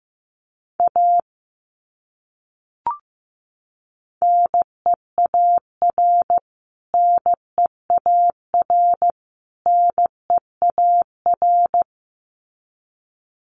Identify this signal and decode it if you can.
Morse code, heard as a keyed tone, about 15 wpm, 700 Hz. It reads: A NEAR NEAR NEAR
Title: Top 200 Words Repeated 15wpm